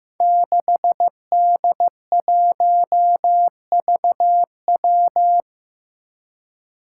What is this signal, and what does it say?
Morse code, heard as a keyed tone, about 15 wpm, 700 Hz. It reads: 6D1VW
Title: Five Challenging Letter-Number Combinations 15wpm